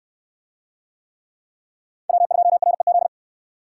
Morse code, heard as a keyed tone, about 63 wpm, 700 Hz. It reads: W2REZ